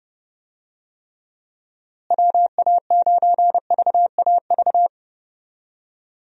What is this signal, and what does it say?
Morse code, heard as a keyed tone, about 30 wpm, 700 Hz. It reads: WA9VAV
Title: USA Call Signs - 6 Characters - Repeated 3x - 30wpm